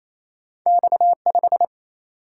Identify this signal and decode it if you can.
Morse code, heard as a keyed tone, about 29 wpm, 700 Hz. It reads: X5
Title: Single-Letter Single-Number 28wpm